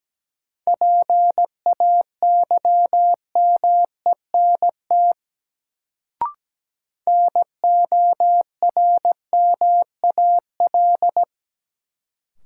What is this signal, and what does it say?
Morse code, heard as a keyed tone, about 17 wpm, 700 Hz. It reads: PAYMENT NORMAL
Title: Sentences from Top 1200 Words - New Word Drill - 17wpm